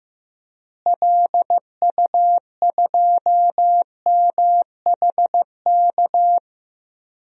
Morse code, heard as a keyed tone, about 15 wpm, 700 Hz. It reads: LU2MHK
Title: DXCC-100 Call Signs - 6 Characters - Repeated 3x - 15wpm